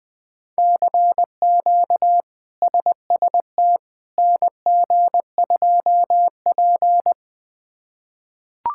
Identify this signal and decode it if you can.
Morse code, heard as a keyed tone, about 20 wpm, 700 Hz. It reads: CQ SST NG2P
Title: K1USN SST QSO Practice - 20wpm